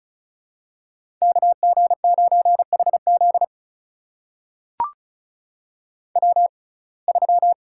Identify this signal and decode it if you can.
Morse code, heard as a keyed tone, about 36 wpm, spaced as for 25 wpm, 700 Hz. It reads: KG9HZ W3
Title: USA Call Signs - 5 Characters - 6x Character Spacing - 35wpm